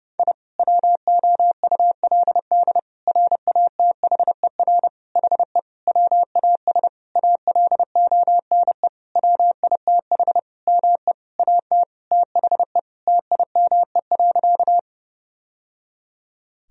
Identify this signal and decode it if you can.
Morse code, heard as a keyed tone, about 32 wpm, 700 Hz. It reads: I WOULD RATHER HE WAS ALONE WITH ME AT THE TIME.
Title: Sentences from Top 1000 Words - Review Part 4 - 30wpm